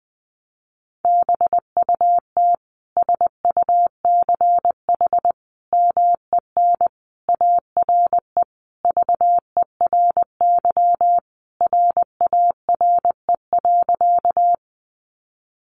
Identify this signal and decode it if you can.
Morse code, heard as a keyed tone, about 20 wpm, 700 Hz. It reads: BUT SUCH MEN ARE VERY RARE.
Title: Sentences from Top 1100 Words 20wpm